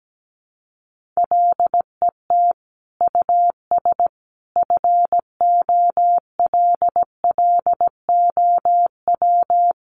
Morse code, heard as a keyed tone, about 17 wpm, 700 Hz. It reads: LET US FOLLOW
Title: Sentences from Top 1400 Words 17wpm